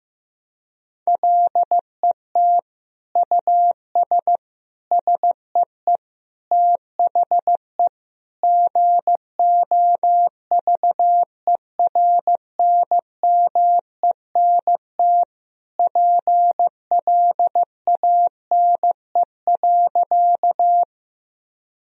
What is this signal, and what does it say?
Morse code, heard as a keyed tone, about 15 wpm, 700 Hz. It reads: LET US SEE THE GOVERNMENT PLANE.
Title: Sentences from Top 1000 Words - Review Part 3 - 15wpm